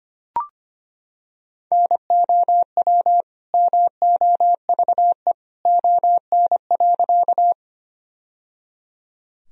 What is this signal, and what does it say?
Morse code, heard as a keyed tone, about 25 wpm, 700 Hz. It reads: NOW MOVE ON.
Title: Sentences from Top 1000 Words - Review Part 1 - 25wpm